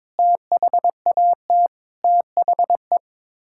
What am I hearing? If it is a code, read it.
Morse code, heard as a keyed tone, about 22 wpm, 700 Hz. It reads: THAT THE
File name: practice / 22wpm / Sets of 8 Words - 22wpm